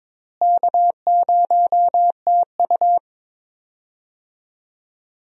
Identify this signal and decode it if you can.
Morse code, heard as a keyed tone, about 22 wpm, 700 Hz. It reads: K0TU
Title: USA Call Signs - 4 Characters - Repeated 2x - 22wpm